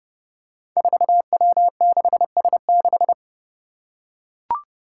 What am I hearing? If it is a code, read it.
Morse code, heard as a keyed tone, about 30 wpm, 700 Hz. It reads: 4W6S6
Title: Five Challenging Letter-Number Combinations 30wpm